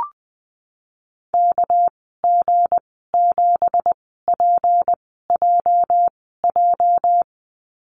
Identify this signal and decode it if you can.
Morse code, heard as a keyed tone, about 20 wpm, spaced as for 15 wpm, 700 Hz. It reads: KG7PJJ
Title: 207 Lesson 33g - Letter j - Call Signs 20wpm